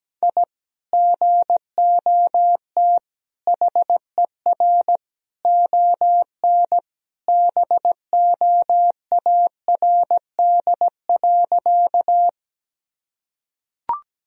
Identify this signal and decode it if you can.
Morse code, heard as a keyed tone, about 17 wpm, 700 Hz. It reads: I GOT HER ON BOARD.
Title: Sentences from Top 1000 Words - Review Part 4 - 17wpm